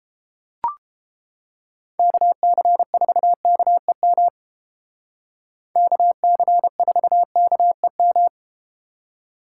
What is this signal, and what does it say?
Morse code, heard as a keyed tone, about 31 wpm, 700 Hz. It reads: KC4KEM KC4KEM